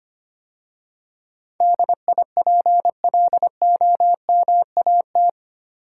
Morse code, heard as a keyed tone, about 25 wpm, 700 Hz. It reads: DIPLOMAT
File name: 8 Letter Words - First 3 Letters Given - 25wpm